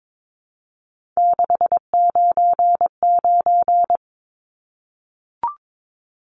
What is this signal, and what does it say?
Morse code, heard as a keyed tone, about 22 wpm, 700 Hz. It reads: 699
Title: Numbers - 3-Digit 22wpm